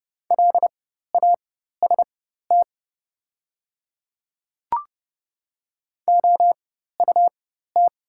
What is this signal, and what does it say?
Morse code, heard as a keyed tone, about 30 wpm, spaced as for 14 wpm, 700 Hz. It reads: LAST OUT
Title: 4x Character Spacing - Top 500 Words - 30wpm